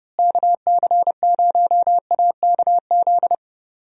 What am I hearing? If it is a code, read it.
Morse code, heard as a keyed tone, about 31 wpm, 700 Hz. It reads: KC0AKZ
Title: USA Call Signs - 6 Characters - Repeated 2x - 30wpm